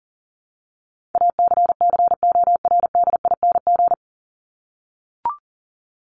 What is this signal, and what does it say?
Morse code, heard as a keyed tone, about 40 wpm, 700 Hz. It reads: ACCORDING
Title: LY Suffix - Top 100 Words with Base Words 40wpm